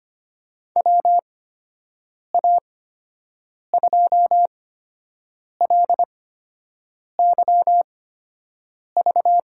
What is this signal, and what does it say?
Morse code, heard as a keyed tone, about 25 wpm, spaced as for 7 wpm, 700 Hz. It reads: WA2LYV
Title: USA Call Signs - 6 Characters - 8x Character Spacing - 25wpm